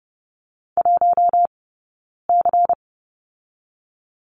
Morse code, heard as a keyed tone, about 30 wpm, spaced as for 9 wpm, 700 Hz. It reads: 1C
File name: Lesson 18d - Letter m - Two Characters 30wpm